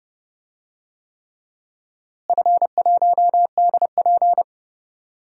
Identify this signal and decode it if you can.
Morse code, heard as a keyed tone, about 31 wpm, 700 Hz. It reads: F1DP
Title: DXCC-100 Call Signs - 4 Characters - Repeated 3x - 30wpm